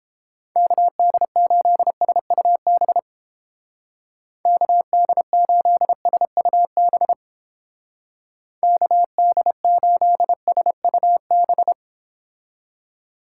Morse code, heard as a keyed tone, about 29 wpm, 700 Hz. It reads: KD8SUB KD8SUB KD8SUB